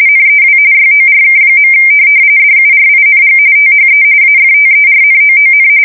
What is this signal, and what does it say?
RTTY, about 45.45 baud. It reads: NEW YORK. DURING THE AFTERNOON
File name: fsk170 / USS-Missouri